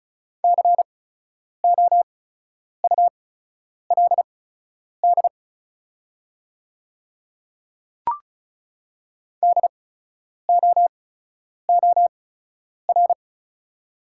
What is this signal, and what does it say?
Morse code, heard as a keyed tone, about 36 wpm, spaced as for 10 wpm, 700 Hz. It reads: COULD DOOR
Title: 8x Character Spacing - Top 300 Words - 35wpm